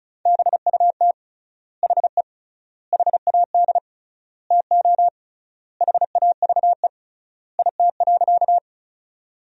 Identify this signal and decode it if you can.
Morse code, heard as a keyed tone, about 36 wpm, 700 Hz. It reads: BUT HE HAD TO HAVE IT.
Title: 3x Word-Spacing - Sentences from Top 100 Words - 35wpm